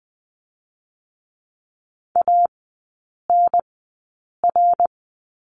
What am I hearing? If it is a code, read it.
Morse code, heard as a keyed tone, about 20 wpm, spaced as for 8 wpm, 700 Hz. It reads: ANR
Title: 053 Lesson 10e - Letter r - Three Characters 20wpm